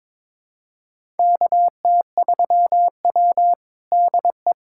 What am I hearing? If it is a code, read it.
Morse code, heard as a keyed tone, about 22 wpm, 700 Hz. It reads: KT3W DE
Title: USA Callsign DE Callsign - 4 Characters - 22wpm